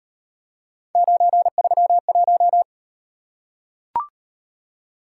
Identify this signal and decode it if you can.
Morse code, heard as a keyed tone, about 38 wpm, 700 Hz. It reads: 931